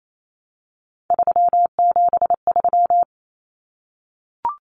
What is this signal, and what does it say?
Morse code, heard as a keyed tone, about 28 wpm, 700 Hz. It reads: 373
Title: Numbers - 3-Digit 28wpm